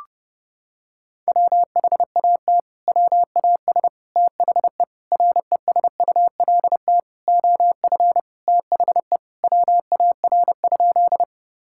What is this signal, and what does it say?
Morse code, heard as a keyed tone, about 32 wpm, 700 Hz. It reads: WHAT WAS THE RESULT OF THE WAR?
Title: Sentences from Top 1000 Words - Review Part 5 - 30wpm